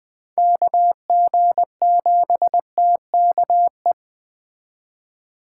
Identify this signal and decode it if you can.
Morse code, heard as a keyed tone, about 20 wpm, 700 Hz. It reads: KG7TKE